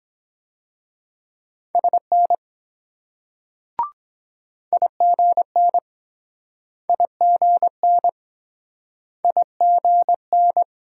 Morse code, heard as a keyed tone, about 24 wpm, 700 Hz. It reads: SN IGN IGN IGN